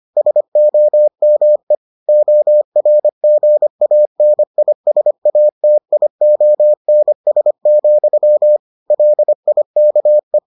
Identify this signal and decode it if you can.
Morse code, heard as a keyed tone, about 25 wpm, 600 Hz. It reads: SOME ORGANISATIONS, LIKE